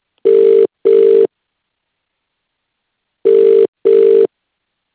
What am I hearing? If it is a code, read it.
Ringback tone (UK), 400 + 450 Hz.